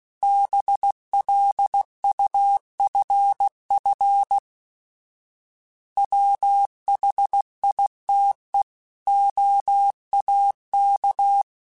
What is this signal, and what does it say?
Morse code, heard as a keyed tone, about 16 wpm, 800 Hz. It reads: BLUFF WHITE OAK
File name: fromSMs / FromSMs mp3s / Cities@15